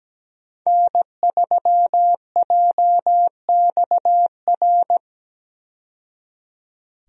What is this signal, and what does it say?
Morse code, heard as a keyed tone, about 17 wpm, 700 Hz. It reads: N3JXR
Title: USA Call Signs - 5 Characters - 17wpm